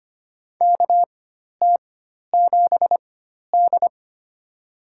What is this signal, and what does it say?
Morse code, heard as a keyed tone, about 25 wpm, spaced as for 12 wpm, 700 Hz. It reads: KT7D
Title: USA Call Signs - 4x Character Spacing - 25wpm